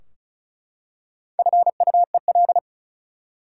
Morse code, heard as a keyed tone, about 36 wpm, 700 Hz. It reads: FUEL
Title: Four Letter Words - 35wpm